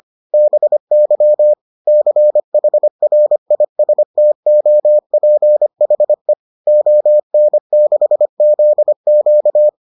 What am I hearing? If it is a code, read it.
Morse code, heard as a keyed tone, about 25 wpm, 600 Hz. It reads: BY CHRISTOPHE ON6ZQ